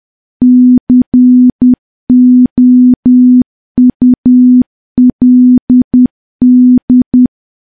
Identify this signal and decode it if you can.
Morse code, heard as a keyed tone, about 10 wpm, 250 Hz. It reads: COULD
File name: morse / morse-training-10WPM